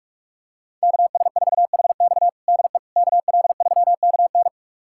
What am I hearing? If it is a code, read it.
Morse code, heard as a keyed tone, about 46 wpm, 700 Hz. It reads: KS4HX DE KL3KN
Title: USA Callsign DE Callsign - 5 Characters - 45wpm